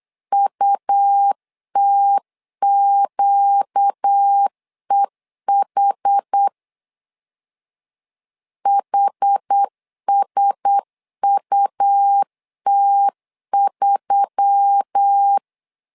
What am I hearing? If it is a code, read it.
Morse code, heard as a keyed tone, about 8 wpm, 800 Hz. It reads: UTQEH HSUT3
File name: LEZIONI / 5 LBVC / 5 L 8wpm 2831